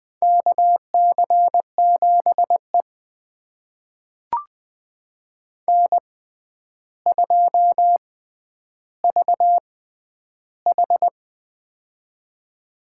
Morse code, heard as a keyed tone, about 20 wpm, spaced as for 10 wpm, 700 Hz. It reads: KC7E N2VH
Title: USA Call Signs - 4 Characters - 6x Character Spacing - 20wpm